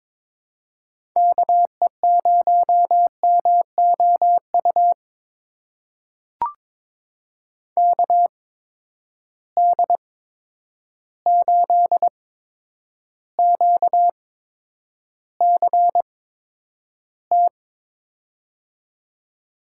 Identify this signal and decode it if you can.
Morse code, heard as a keyed tone, about 22 wpm, spaced as for 9 wpm, 700 Hz. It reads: KE0MOU KD8QCT
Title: USA Call Signs - 6 Characters - 8x Character Spacing - 22wpm